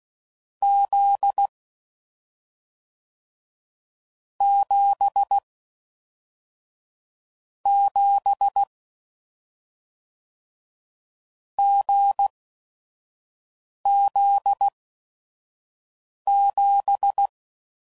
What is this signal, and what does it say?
Morse code, heard as a keyed tone, about 16 wpm, spaced as for 5 wpm, 800 Hz. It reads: Z 7 7 GZ7
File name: Combinations / Difficult Combos / awj1 ndb6 gz7